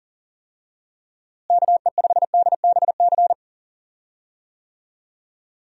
Morse code, heard as a keyed tone, about 40 wpm, 700 Hz. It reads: KE5DBC
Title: USA Call Signs - 6 Characters - Repeated 2x - 40wpm